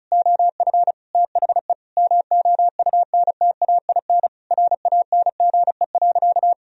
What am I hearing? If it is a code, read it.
Morse code, heard as a keyed tone, about 36 wpm, 700 Hz. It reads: OF THE MOUNTAIN RANGE.
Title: Sentences from Top 1000 Words - Review Part 4 - 35wpm